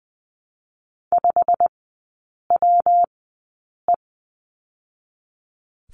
Morse code, heard as a keyed tone, about 20 wpm, 700 Hz. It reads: 5 W E
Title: 121 Lesson 20e - Number 3 - Three Characters 20wpm